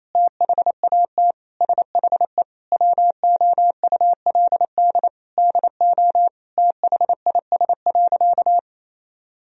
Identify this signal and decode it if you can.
Morse code, heard as a keyed tone, about 28 wpm, 700 Hz. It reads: THAT SHE WOULD DO THIS.
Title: Sentences from Top 1000 Words - Review Part 3 - 28wpm